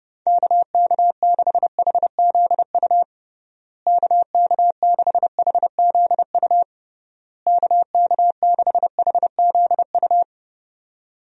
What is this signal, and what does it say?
Morse code, heard as a keyed tone, about 32 wpm, 700 Hz. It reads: KK6HZU KK6HZU KK6HZU
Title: USA Call Signs - 6 Characters - Repeated 3x - 30wpm